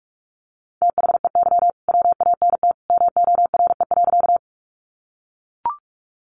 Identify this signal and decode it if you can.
Morse code, heard as a keyed tone, about 44 wpm, 700 Hz. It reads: THEY WANT MORE.
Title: Sentences from Top 200 Words 45wpm